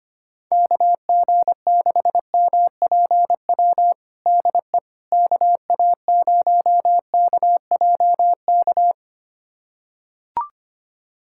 Morse code, heard as a keyed tone, about 25 wpm, 700 Hz. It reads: KG6MPW DE KA0KJK
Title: USA Callsign DE Callsign - 6 Characters - 25wpm